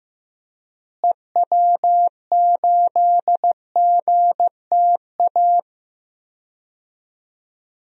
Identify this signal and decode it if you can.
Morse code, heard as a keyed tone, about 15 wpm, 700 Hz. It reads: EW8GTA
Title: DXCC-100 Call Signs - 6 Characters - Repeated 3x - 15wpm